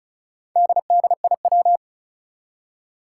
Morse code, heard as a keyed tone, about 36 wpm, 700 Hz. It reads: DDIW
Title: Four Challenging Letter-Number Combinations 35wpm